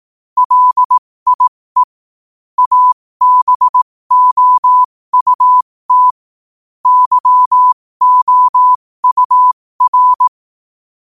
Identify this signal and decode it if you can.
Morse code, heard as a keyed tone, about 18 wpm, spaced as for 15 wpm, 1000 Hz. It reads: LIE ABOUT YOUR
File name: qotdfiles / 15wpm